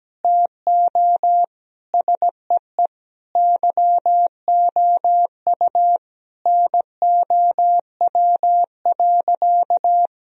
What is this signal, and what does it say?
Morse code, heard as a keyed tone, about 17 wpm, 700 Hz. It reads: TO SEE YOU NOW.
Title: Sentences from Top 1000 Words - Review Part 2 - 17wpm